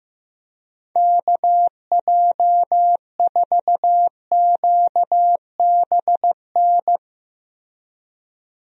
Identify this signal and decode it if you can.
Morse code, heard as a keyed tone, about 15 wpm, 700 Hz. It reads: KJ4QBN
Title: USA Call Signs - 6 Characters - Repeated 2x - 15wpm